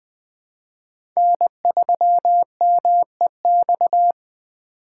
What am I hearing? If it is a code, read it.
Morse code, heard as a keyed tone, about 20 wpm, 700 Hz. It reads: N3MEX
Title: USA Call Signs - 5 Characters - 4x Character Spacing - 20wpm